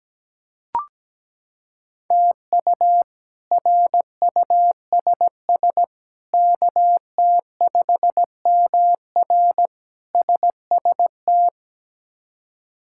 Morse code, heard as a keyed tone, about 17 wpm, 700 Hz. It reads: TU RUSS KT5MR SST
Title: K1USN SST QSO Practice - 17wpm